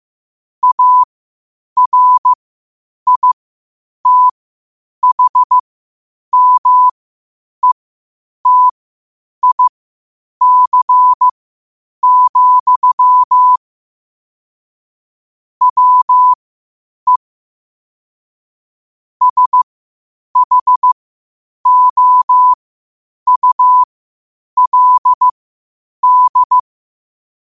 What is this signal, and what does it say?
Morse code, heard as a keyed tone, about 15 wpm, spaced as for 8 wpm, 1000 Hz. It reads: ARITHMETIC, WE SHOULD
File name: qotdfiles / 8wpm